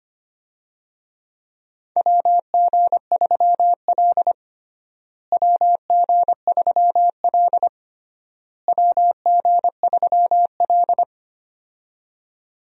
Morse code, heard as a keyed tone, about 25 wpm, 700 Hz. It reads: WG3L WG3L WG3L
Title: USA Call Signs - 4 Characters - Repeated 3x - 25wpm